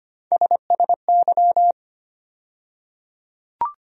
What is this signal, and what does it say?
Morse code, heard as a keyed tone, about 25 wpm, 700 Hz. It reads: SSY
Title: USA Call Sign Postfixes - 3 Letters 25wpm